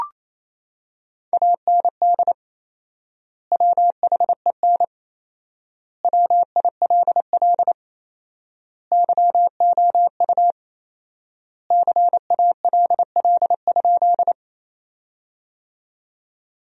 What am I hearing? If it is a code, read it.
Morse code, heard as a keyed tone, about 28 wpm, 700 Hz. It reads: AND WHEN WILL YOU CALL?
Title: 4x Word-Spacing - Sentences from Top 100 Words - 28wpm